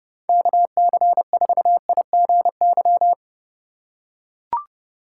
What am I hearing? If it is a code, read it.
Morse code, heard as a keyed tone, about 29 wpm, 700 Hz. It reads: KC4IGY